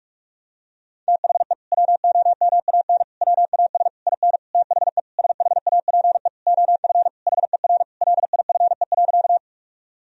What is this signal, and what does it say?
Morse code, heard as a keyed tone, about 41 wpm, 700 Hz. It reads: THE WOMAN WAS IN THE SHAPE OF HER LIFE.
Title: Sentences from Top 1000 Words - Review Part 3 - 45wpm